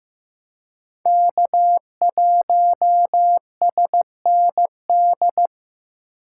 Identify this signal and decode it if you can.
Morse code, heard as a keyed tone, about 15 wpm, 700 Hz. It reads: K1SND
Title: K1USN SST QSO Practice - 15wpm